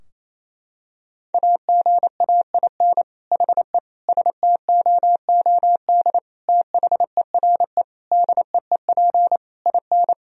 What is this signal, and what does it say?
Morse code, heard as a keyed tone, about 28 wpm, 700 Hz. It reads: AGAIN HE STOOD THERE DEEP IN
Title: Sentences from Top 1000 Words - Review Part 3 - 28wpm